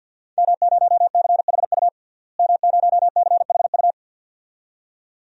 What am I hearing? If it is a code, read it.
Morse code, heard as a keyed tone, about 52 wpm, 700 Hz. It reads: M0CHU M0CHU
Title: DXCC-100 Call Signs - 5 Characters - Repeated 2x - 50wpm